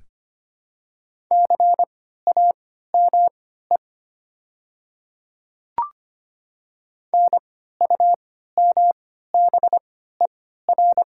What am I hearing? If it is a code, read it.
Morse code, heard as a keyed tone, about 25 wpm, spaced as for 14 wpm, 690 Hz. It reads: CAME NUMBER
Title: 3x Character Spacing - Top 200 Words - 25wpm